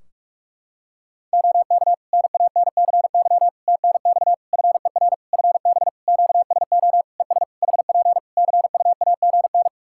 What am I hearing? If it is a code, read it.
Morse code, heard as a keyed tone, about 46 wpm, 700 Hz. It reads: OK NANCY TNX FER FB QSO ES HP CUAGN
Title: Practice QSO - Exchange 4 - Ending - 45wpm